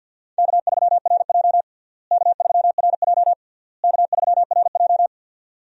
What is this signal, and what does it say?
Morse code, heard as a keyed tone, about 47 wpm, 700 Hz. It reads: K3RJ K3RJ K3RJ
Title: USA Call Signs - 4 Characters - Repeated 3x - 50wpm